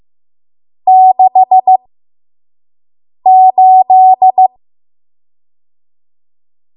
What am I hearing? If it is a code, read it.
Morse code, heard as a keyed tone, about 15 wpm, spaced as for 5 wpm, 750 Hz. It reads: 68
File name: Morse / 05WPM